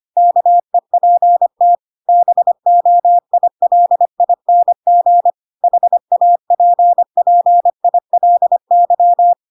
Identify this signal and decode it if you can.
Morse code, heard as a keyed tone, about 25 wpm, 700 Hz. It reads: KEPT BOILING HAPPILY
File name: Pinocchio / Pinocchio ch3 25 25 ik1hge